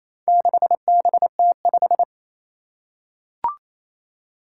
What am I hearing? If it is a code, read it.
Morse code, heard as a keyed tone, about 28 wpm, 700 Hz. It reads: 6BT5
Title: Four Challenging Letter-Number Combinations 28wpm